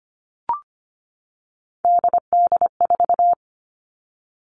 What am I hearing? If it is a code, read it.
Morse code, heard as a keyed tone, about 25 wpm, 700 Hz. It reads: DD4
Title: Three Challenging Letter-Number Combinations 25wpm